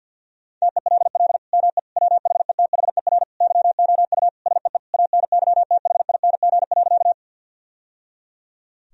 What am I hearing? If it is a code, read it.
Morse code, heard as a keyed tone, about 50 wpm, 700 Hz. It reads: TELL ME WHETHER YOU SEE ANYTHING.
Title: Sentences from Top 1000 Words - Review Part 3 - 50wpm